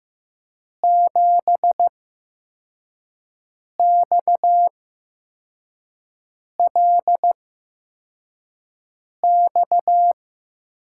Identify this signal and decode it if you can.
Morse code, heard as a keyed tone, about 15 wpm, spaced as for 4 wpm, 700 Hz. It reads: 7XLX
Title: USA Call Signs - 5 Characters - 8x Character Spacing - 15wpm